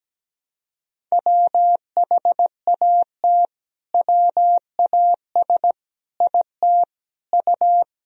Morse code, heard as a keyed tone, about 17 wpm, 700 Hz. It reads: WHAT WAS IT U
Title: Sentences from Top 200 Words and Abbreviations 17wpm